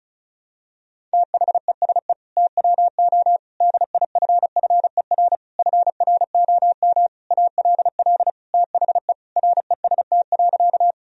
Morse code, heard as a keyed tone, about 34 wpm, 700 Hz. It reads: THESE TWO DIFFER FROM ALL THE REST.
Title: Sentences from Top 1000 Words - Review Part 1 - 35wpm